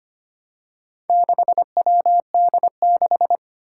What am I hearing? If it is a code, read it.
Morse code, heard as a keyed tone, about 25 wpm, 700 Hz. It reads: 6WD6
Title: Four Challenging Letter-Number Combinations 25wpm